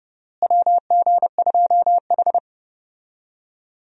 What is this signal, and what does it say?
Morse code, heard as a keyed tone, about 31 wpm, 700 Hz. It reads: WG2H
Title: USA Call Signs - 4 Characters - Repeated 2x - 30wpm